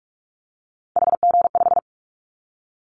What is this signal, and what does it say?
Morse code, heard as a keyed tone, about 46 wpm, 700 Hz. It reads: HG5
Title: Callsign Prefixes for Top 50 DXCC - 45wpm